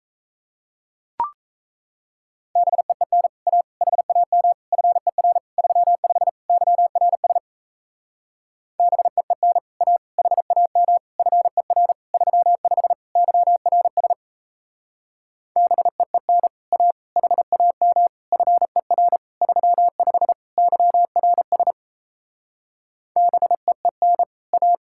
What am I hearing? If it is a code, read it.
Morse code, heard as a keyed tone, about 37 wpm, 700 Hz. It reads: BEEN A HAM FER 35 YRS BEEN A HAM FER 35 YRS BEEN A HAM FER 35 YRS BEEN A